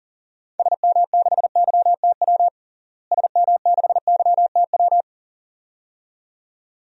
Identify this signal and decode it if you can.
Morse code, heard as a keyed tone, about 39 wpm, 700 Hz. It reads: SM6YTW SM6YTW
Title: DXCC-100 Call Signs - 6 Characters - Repeated 2x - 40wpm